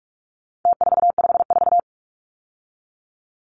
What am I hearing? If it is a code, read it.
Morse code, heard as a keyed tone, about 44 wpm, 700 Hz. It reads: T454
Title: Numbers - Contest Style - 4-Digit 45wpm